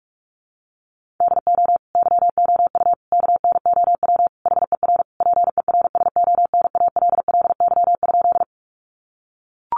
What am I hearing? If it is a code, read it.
Morse code, heard as a keyed tone, about 44 wpm, 700 Hz. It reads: DO YOU KNOW HER PERSONALLY?
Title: Sentences from Top 1400 Words 45wpm